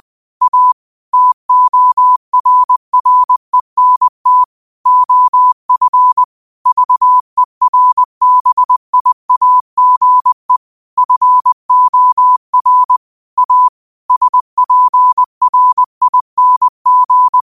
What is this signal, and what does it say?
Morse code, heard as a keyed tone, about 20 wpm, 1000 Hz. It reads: A TORRENT OF VERBIAGE FOR A SPRING
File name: qotdfiles / 20wpm